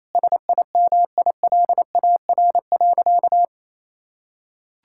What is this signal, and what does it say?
Morse code, heard as a keyed tone, about 28 wpm, 700 Hz. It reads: SIMILAR.
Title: Sentences from Top 1000 Words - Review Part 5 - 28wpm